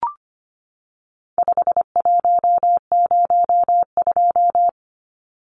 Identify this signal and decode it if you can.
Morse code, heard as a keyed tone, about 25 wpm, 700 Hz. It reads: 5102
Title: Numbers - 4-Digit 25wpm